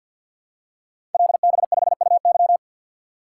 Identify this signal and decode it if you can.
Morse code, heard as a keyed tone, about 50 wpm, 700 Hz. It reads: LB5UY
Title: Code Groups - 5 Letter:Numbers - 50wpm